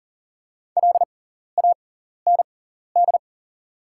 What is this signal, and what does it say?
Morse code, heard as a keyed tone, about 40 wpm, spaced as for 14 wpm, 690 Hz. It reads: LAND
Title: 6x Character Spacing - Top 500 Words - 40wpm